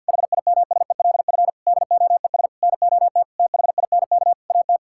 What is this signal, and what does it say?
Morse code, heard as a keyed tone, about 50 wpm, 700 Hz. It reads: HIMSELF DOES NOT THINK AT
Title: Sentences from Top 1600 Words 50wpm